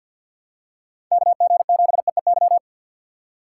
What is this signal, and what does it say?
Morse code, heard as a keyed tone, about 47 wpm, 690 Hz. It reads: KG7EEY
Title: USA Call Signs - 2x Character Spacing - 50wpm